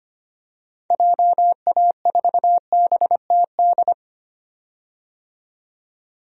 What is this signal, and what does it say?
Morse code, heard as a keyed tone, about 25 wpm, 700 Hz. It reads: JA4BTD
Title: DXCC-100 Call Signs - 6 Characters - 25wpm